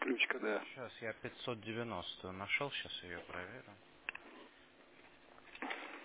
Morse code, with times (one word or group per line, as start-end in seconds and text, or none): none